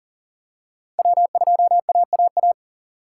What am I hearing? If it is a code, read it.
Morse code, heard as a keyed tone, about 40 wpm, 700 Hz. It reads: W2AAA
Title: USA Call Signs - 5 Characters - 4x Character Spacing - 40wpm